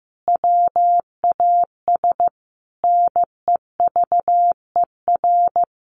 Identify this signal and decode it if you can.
Morse code, heard as a keyed tone, about 15 wpm, 700 Hz. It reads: WAS NEVER
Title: Sentences from Top 700 Words 15wpm